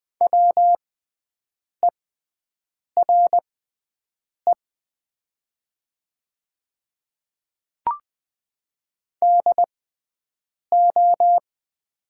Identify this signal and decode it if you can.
Morse code, heard as a keyed tone, about 20 wpm, spaced as for 7 wpm, 700 Hz. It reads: WERE DO
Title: 6x Character Spacing - Top 100 Words - 20wpm